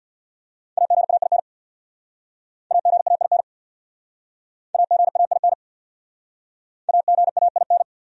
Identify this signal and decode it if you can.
Morse code, heard as a keyed tone, about 61 wpm, 700 Hz. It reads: AGAIN AGAIN AGAIN AGAIN
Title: Speed-Racing Top 500 Words 50wpm